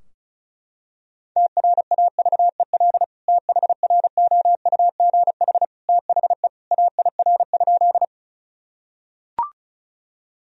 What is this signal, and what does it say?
Morse code, heard as a keyed tone, about 36 wpm, 700 Hz. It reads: TRAVEL THROUGH THE AIR?
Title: Sentences from Top 1000 Words - Review Part 2 - 35wpm